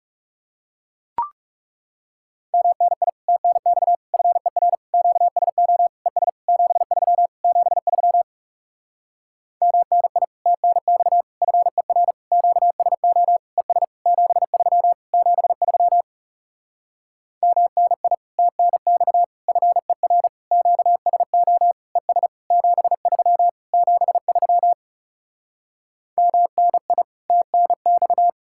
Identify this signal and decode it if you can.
Morse code, heard as a keyed tone, about 40 wpm, 700 Hz. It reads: MNI TNX FER QSO ES 73 73 MNI TNX FER QSO ES 73 73 MNI TNX FER QSO ES 73 73 MNI TNX